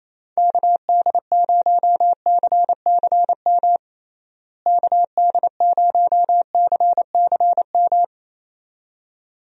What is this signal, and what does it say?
Morse code, heard as a keyed tone, about 27 wpm, 700 Hz. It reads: KD0CCM KD0CCM
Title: USA Call Signs - 6 Characters - Repeated 2x - 28wpm